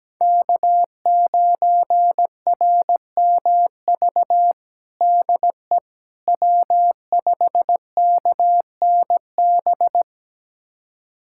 Morse code, heard as a keyed tone, about 17 wpm, 700 Hz. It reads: K9RMV DE W5KNB
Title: USA Callsign DE Callsign - 5 Characters - 17wpm